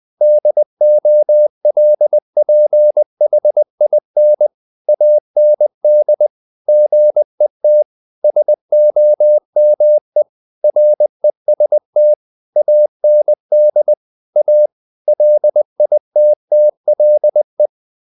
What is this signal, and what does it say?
Morse code, heard as a keyed tone, about 20 wpm, 600 Hz. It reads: DOLPHIN AND GET SOME REST AND A LITTLE
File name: the old man and the sea part 2 20wpm